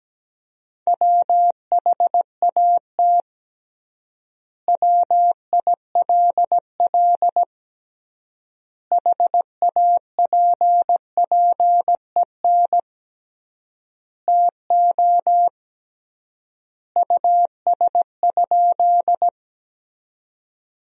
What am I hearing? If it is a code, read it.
Morse code, heard as a keyed tone, about 17 wpm, 700 Hz. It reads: WHAT WILL HAPPEN TO US?
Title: 3x Word-Spacing - Sentences from Top 400 Words - 17wpm